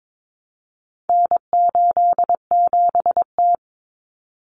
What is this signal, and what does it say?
Morse code, heard as a keyed tone, about 22 wpm, 700 Hz. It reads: N87T
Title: Common Cut Numbers - 4-Digit 22wpm